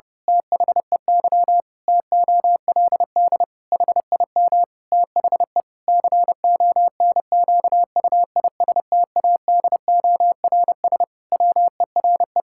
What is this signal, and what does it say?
Morse code, heard as a keyed tone, about 32 wpm, 700 Hz. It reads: THEY TOLD HIM THE CONQUISTADORS WERE
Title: Pangram Challenge v2 - 30wpm